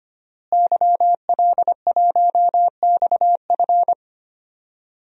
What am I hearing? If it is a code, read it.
Morse code, heard as a keyed tone, about 25 wpm, 700 Hz. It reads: YL1XF
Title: DXCC-100 Call Signs - 5 Characters - Repeated 3x - 25wpm